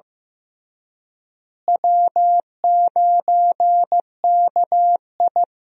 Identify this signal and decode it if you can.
Morse code, heard as a keyed tone, about 15 wpm, 700 Hz. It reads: W9KI
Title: USA Call Signs - 4 Characters - Repeated 3x - 15wpm